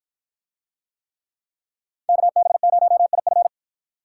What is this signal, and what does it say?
Morse code, heard as a keyed tone, about 51 wpm, 700 Hz. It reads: KB0IF